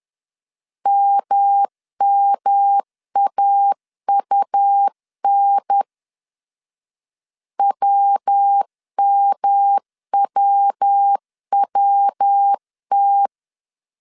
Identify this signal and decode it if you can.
Morse code, heard as a keyed tone, about 10 wpm, 800 Hz. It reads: MMAUN WMWWT
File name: CW / DET / 1 3 1500C S A 10wpm 3447